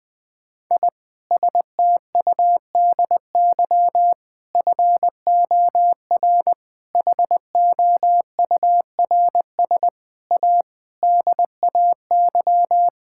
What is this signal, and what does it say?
Morse code, heard as a keyed tone, about 20 wpm, 700 Hz. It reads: I STUDY FOR HOURS A DAY
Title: Sentences with 7-Word Phrases - Spoken Beginning - 20wpm